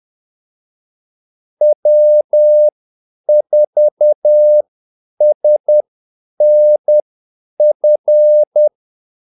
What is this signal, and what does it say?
Morse code, heard as a keyed tone, about 10 wpm, spaced as for 8 wpm, 600 Hz. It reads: W4SNF
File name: Callsigns-10-8wpm